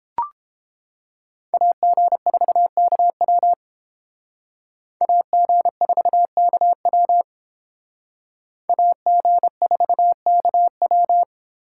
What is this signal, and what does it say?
Morse code, heard as a keyed tone, about 30 wpm, 700 Hz. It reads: AG4KW AG4KW AG4KW